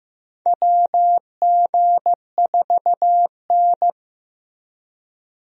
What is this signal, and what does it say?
Morse code, heard as a keyed tone, about 15 wpm, 700 Hz. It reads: WG4N
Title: USA Call Signs - 4 Characters - Repeated 3x - 15wpm